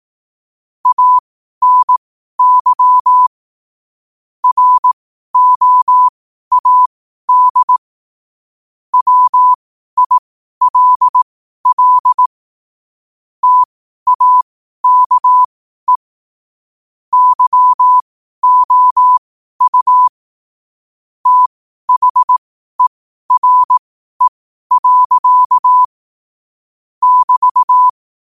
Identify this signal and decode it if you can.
Morse code, heard as a keyed tone, about 18 wpm, spaced as for 12 wpm, 1000 Hz. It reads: ANY ROAD WILL TAKE YOU THERE. <BT>
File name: qotdfiles / 12wpm